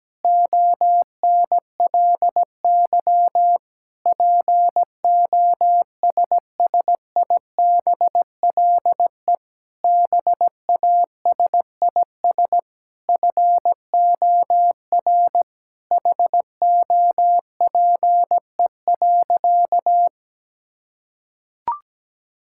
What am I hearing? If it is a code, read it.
Morse code, heard as a keyed tone, about 17 wpm, 700 Hz. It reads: ONLY POSSIBLE BASIS FOR HOPE.
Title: Sentences from Top 1000 Words - Review Part 5 - 17wpm